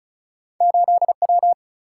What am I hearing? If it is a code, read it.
Morse code, heard as a keyed tone, about 33 wpm, 700 Hz. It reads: 8 W
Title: Single-Letter Single-Number 35wpm